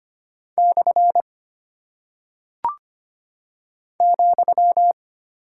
Morse code, heard as a keyed tone, about 25 wpm, 700 Hz. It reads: / ,
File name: Single Punctuation-Prosign - 25wpm